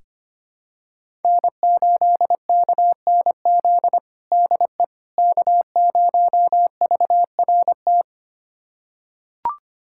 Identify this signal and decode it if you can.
Morse code, heard as a keyed tone, about 25 wpm, 700 Hz. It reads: N8KNZ DE K0VRT
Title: USA Callsign DE Callsign - 5 Characters - 25wpm